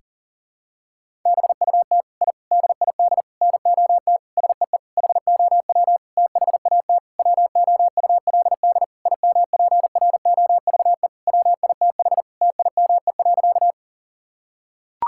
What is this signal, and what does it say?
Morse code, heard as a keyed tone, about 40 wpm, 700 Hz. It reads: BUT I DID NOT SEE HOW THAT WOULD IMPROVE WITH TIME.
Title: Sentences from Top 1000 Words - Review Part 5 - 40wpm